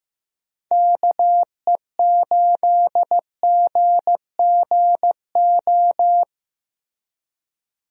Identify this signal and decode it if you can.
Morse code, heard as a keyed tone, about 15 wpm, 700 Hz. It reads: KE8GGO
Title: USA Call Signs - 6 Characters - Repeated 2x - 15wpm